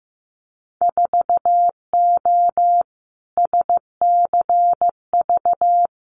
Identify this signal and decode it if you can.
Morse code, heard as a keyed tone, about 15 wpm, 700 Hz. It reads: 4O SCV
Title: Winter Field Day Exchanges 15wpm